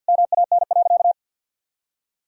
Morse code, heard as a keyed tone, about 50 wpm, 700 Hz. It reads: MAN.
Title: Sentences from Top 1000 Words - Review Part 3 - 50wpm